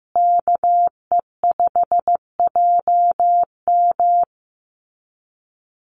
Morse code, heard as a keyed tone, about 15 wpm, 700 Hz. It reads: KE5JM
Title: USA Call Signs Repeated 2x 15wpm